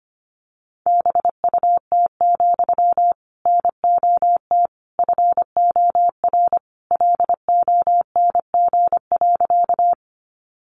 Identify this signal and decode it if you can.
Morse code, heard as a keyed tone, about 25 wpm, 700 Hz. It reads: BUT, NOT FOR LONG.
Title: Sentences from Top 600 Words 25wpm